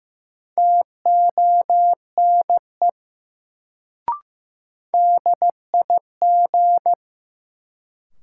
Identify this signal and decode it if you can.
Morse code, heard as a keyed tone, about 15 wpm, 700 Hz. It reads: TONE DIG
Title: Sentences from Top 1600 Words - New Word Drill Alt - 15wpm